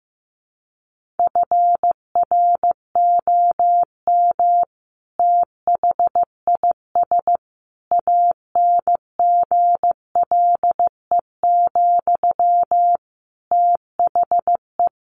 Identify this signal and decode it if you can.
Morse code, heard as a keyed tone, about 14 wpm, 700 Hz. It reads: FROM THIS ANGLE, THE
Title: Sentences from Top 1500 Words 15wpm